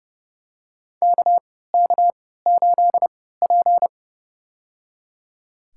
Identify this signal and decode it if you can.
Morse code, heard as a keyed tone, about 30 wpm, spaced as for 17 wpm, 700 Hz. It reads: KK8P
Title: USA Call Signs - 4 Characters - 3x Character Spacing - 30wpm